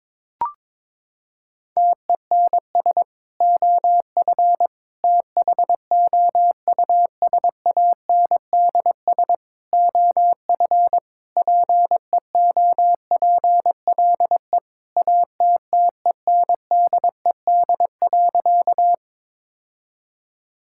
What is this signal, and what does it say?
Morse code, heard as a keyed tone, about 22 wpm, 700 Hz. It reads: TENS OF THOUSANDS OF PEOPLE ATTENDED.
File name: practice / 22wpm / Sentences with 4-Word Phrases - 22wpm